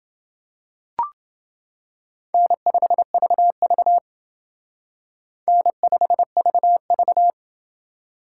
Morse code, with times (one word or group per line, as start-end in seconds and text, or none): 2.3-4.0 N5VV
5.5-7.3 N5VV